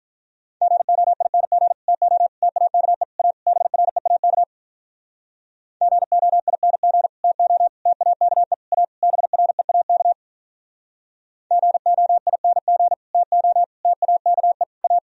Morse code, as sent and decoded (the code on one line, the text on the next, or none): --. --- .. -. --. / - --- / - .- -.- . / .- / -... .-. .- -.- / --. --- .. -. --. / - --- / - .- -.- . / .- / -... .-. .- -.- / --. --- .. -. --. / - --- / - .- -.- . / .-
GOING TO TAKE A BREAK GOING TO TAKE A BREAK GOING TO TAKE A